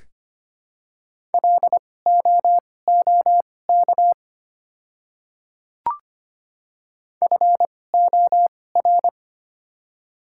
Morse code, heard as a keyed tone, about 25 wpm, spaced as for 18 wpm, 700 Hz. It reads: LOOK FOR
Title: 2x Character Spacing - Top 100 Words - 25wpm